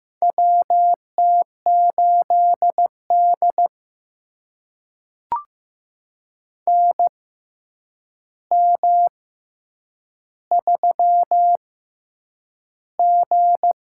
Morse code, heard as a keyed tone, about 15 wpm, spaced as for 8 wpm, 700 Hz. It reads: WT8D NM3G
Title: USA Call Signs - 4 Characters - 6x Character Spacing - 15wpm